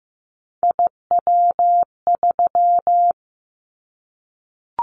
Morse code, heard as a keyed tone, about 15 wpm, 700 Hz. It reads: IW3
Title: Callsign Prefixes for Top 25 DXCC - Repeated 3x - 15wpm